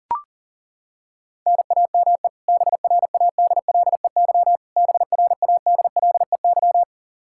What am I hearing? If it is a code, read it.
Morse code, heard as a keyed tone, about 37 wpm, 700 Hz. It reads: NAME BRADLEY BRADLEY
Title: Practice QSO - Exchange 1 - Beginning - 40wpm